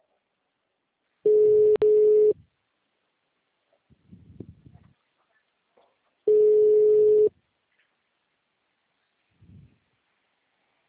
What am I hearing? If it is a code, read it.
Ringback tone (Europe, 425 Hz).
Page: Marilia - SP